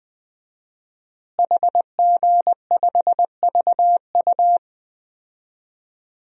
Morse code, heard as a keyed tone, about 20 wpm, 700 Hz. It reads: HG5VU
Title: DXCC-100 Call Signs - 5 Characters - Repeated 3x - 20wpm